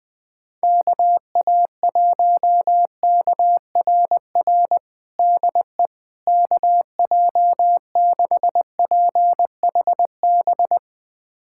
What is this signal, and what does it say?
Morse code, heard as a keyed tone, about 20 wpm, 700 Hz. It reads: KA1KRR DE KJ6PHB
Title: USA Callsign DE Callsign - 6 Characters - 20wpm